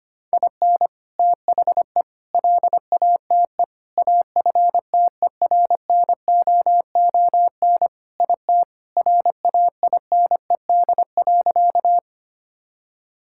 Morse code, heard as a keyed tone, about 25 wpm, 700 Hz. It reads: IN THE LATE AFTERNOON IT RAINED.
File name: practice / 25wpm / Sentences with 3-Word Phrases - 25wpm